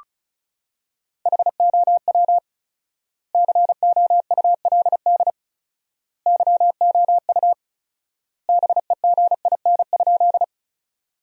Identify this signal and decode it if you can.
Morse code, heard as a keyed tone, about 34 wpm, 700 Hz. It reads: HOW COULD YOU BEGIN?
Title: 4x Word-Spacing - Sentences from Top 300 Words - 35wpm